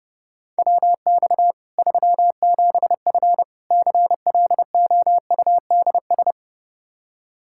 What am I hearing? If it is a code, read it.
Morse code, heard as a keyed tone, about 30 wpm, 700 Hz. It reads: WX 37F CLOUDS
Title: Practice QSO - Exchange 2 - Info - 30wpm